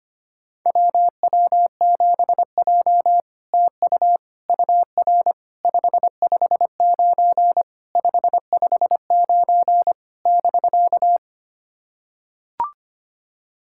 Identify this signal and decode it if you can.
Morse code, heard as a keyed tone, about 25 wpm, 690 Hz. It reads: WW7J TU UR 559 559 <BK>
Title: SOTA QSO Practice 25wpm